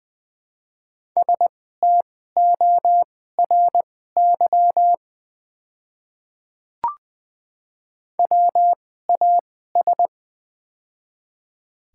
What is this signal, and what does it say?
Morse code, heard as a keyed tone, about 20 wpm, spaced as for 15 wpm, 700 Hz. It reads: STORY WAS
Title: 2x Character Spacing - Top 300 Words - 20wpm